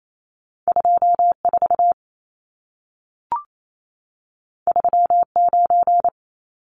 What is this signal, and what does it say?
Morse code, heard as a keyed tone, about 28 wpm, 700 Hz. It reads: 24 39
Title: Numbers - 2-Digit 28wpm